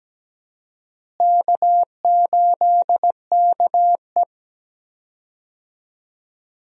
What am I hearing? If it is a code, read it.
Morse code, heard as a keyed tone, about 17 wpm, 700 Hz. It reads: K8KE
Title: USA Call Signs - 4 Characters - Repeated 3x - 17wpm